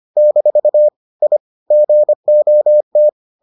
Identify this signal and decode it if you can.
Morse code, heard as a keyed tone, about 25 wpm, 600 Hz. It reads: - I GOT